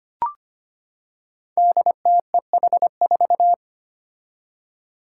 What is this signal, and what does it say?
Morse code, heard as a keyed tone, about 25 wpm, 700 Hz. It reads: DTEH4
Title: Five Challenging Letter-Number Combinations 25wpm